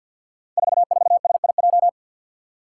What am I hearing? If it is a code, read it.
Morse code, heard as a keyed tone, about 52 wpm, 700 Hz. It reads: 44SIJ
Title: Five Challenging Letter-Number Combinations 50wpm